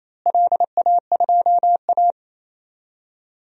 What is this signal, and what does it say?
Morse code, heard as a keyed tone, about 28 wpm, 700 Hz. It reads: LA2A
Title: DXCC-100 Call Signs - 4 Characters - Repeated 2x - 28wpm